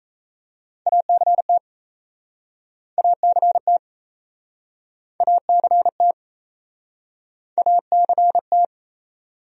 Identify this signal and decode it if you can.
Morse code, heard as a keyed tone, about 39 wpm, 700 Hz. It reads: ACT ACT ACT ACT